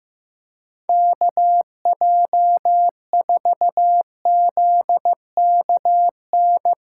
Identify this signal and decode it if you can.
Morse code, heard as a keyed tone, about 15 wpm, 700 Hz. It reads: KJ4ZKN
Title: USA Callsign DE Callsign - 6 Characters - 15wpm